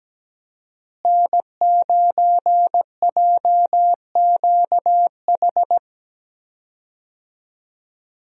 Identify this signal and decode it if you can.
Morse code, heard as a keyed tone, about 17 wpm, 700 Hz. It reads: N9JQH
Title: USA Call Signs - 5 Characters - Repeated 2x - 17wpm